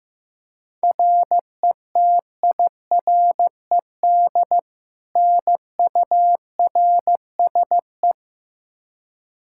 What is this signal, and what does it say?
Morse code, heard as a keyed tone, about 15 wpm, 700 Hz. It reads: RETIRED NURSE
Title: Practice QSO - Exchange 3 - More Info - 15wpm